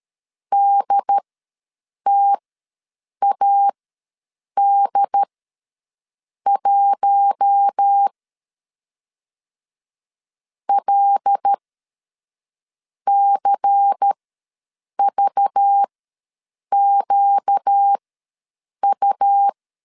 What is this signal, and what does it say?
Morse code, heard as a keyed tone, about 13 wpm, spaced as for 6 wpm, 800 Hz. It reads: DTAD1 LCVQU
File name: LEZIONI / 5 LBVC / 5 LBVC ws 12wpm 3159